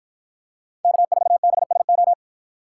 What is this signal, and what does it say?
Morse code, heard as a keyed tone, about 53 wpm, 700 Hz. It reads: K4BSO